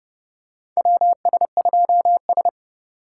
Morse code, heard as a keyed tone, about 30 wpm, 700 Hz. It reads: WS2S
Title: USA Call Signs - 4 Characters - 3x Character Spacing - 30wpm